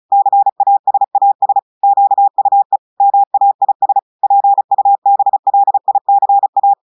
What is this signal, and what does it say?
Morse code, heard as a keyed tone, about 36 wpm, 800 Hz. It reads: CASAS QUE MAIS PUBLICA
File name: audios / portugais / portugais texte en clair 1 vitesse 35 mots minutes